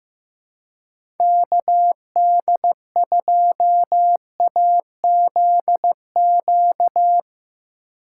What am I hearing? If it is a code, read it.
Morse code, heard as a keyed tone, about 15 wpm, 700 Hz. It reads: KD2AZQ
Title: USA Call Signs - 6 Characters - 2x Character Spacing - 15wpm